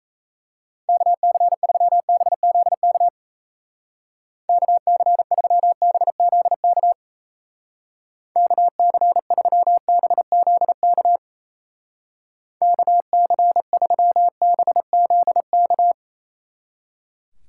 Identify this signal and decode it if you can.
Morse code, heard as a keyed tone, about 35 wpm, 700 Hz. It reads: KC3BZK KC3BZK KC3BZK KC3BZK
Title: Speed Racing - USA 2x3 Call Signs 28wpm